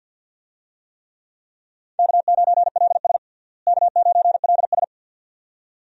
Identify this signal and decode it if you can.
Morse code, heard as a keyed tone, about 51 wpm, 700 Hz. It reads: K9LS K9LS
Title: USA Call Signs - 4 Characters - Repeated 2x - 50wpm